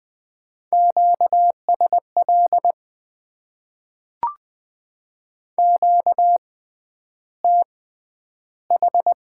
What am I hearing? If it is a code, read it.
Morse code, heard as a keyed tone, about 20 wpm, spaced as for 10 wpm, 700 Hz. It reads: QSL QTH
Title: Top 90 QSO Elements - 6x Character Spacing - 20wpm